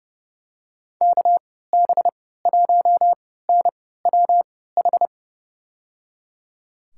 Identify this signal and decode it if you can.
Morse code, heard as a keyed tone, about 31 wpm, spaced as for 17 wpm, 700 Hz. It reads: KB1NWH
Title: USA Call Signs - 6 Characters - 3x Character Spacing - 30wpm